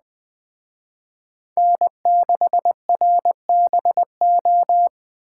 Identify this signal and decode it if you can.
Morse code, heard as a keyed tone, about 20 wpm, 700 Hz. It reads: N6RBO
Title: Speed Racing - USA Call Signs 17wpm